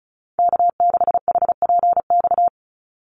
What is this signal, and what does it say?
Morse code, heard as a keyed tone, about 34 wpm, 700 Hz. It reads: K6HPX
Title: NA SOTA Chaser Callsigns 35wpm